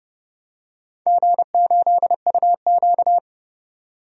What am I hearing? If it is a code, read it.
Morse code, heard as a keyed tone, about 30 wpm, 700 Hz. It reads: G8UQ
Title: DXCC-100 Call Signs - 4 Characters - 30wpm